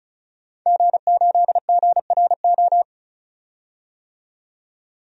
Morse code, heard as a keyed tone, about 36 wpm, 700 Hz. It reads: G8GRO
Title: DXCC-100 Call Signs - 5 Characters - 35wpm